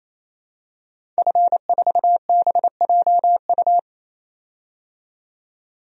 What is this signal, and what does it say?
Morse code, heard as a keyed tone, about 28 wpm, 700 Hz. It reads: F4BJU
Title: DXCC-100 Call Signs - 5 Characters - 28wpm